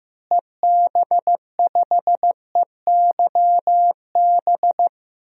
Morse code, heard as a keyed tone, about 15 wpm, 700 Hz. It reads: EB5EYB
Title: DXCC-100 Call Signs - 6 Characters - Repeated 2x - 15wpm